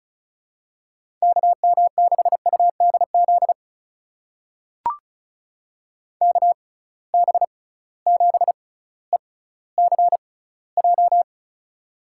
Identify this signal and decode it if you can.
Morse code, heard as a keyed tone, about 36 wpm, spaced as for 18 wpm, 700 Hz. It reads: KM6UDZ KB7ECJ
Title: USA Call Signs - 6 Characters - 6x Character Spacing - 35wpm